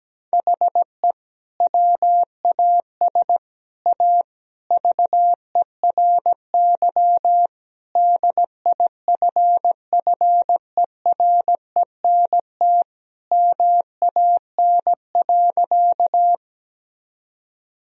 Morse code, heard as a keyed tone, about 17 wpm, 700 Hz. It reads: HE WAS A VERY DIFFERENT MAN.
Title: Sentences from Top 1000 Words - Review Part 3 - 17wpm